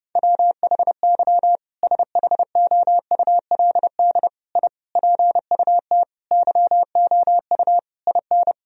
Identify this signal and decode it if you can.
Morse code, heard as a keyed tone, about 32 wpm, 700 Hz. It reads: WHY SHOULD I PUT YOU IN
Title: Sentences from Top 1000 Words - Review Part 4 - 30wpm